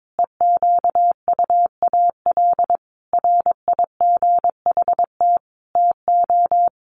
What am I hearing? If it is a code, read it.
Morse code, heard as a keyed tone, about 22 wpm, 700 Hz. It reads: EQUAL RIGHT TO
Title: Sentences from Top 1400 Words 22wpm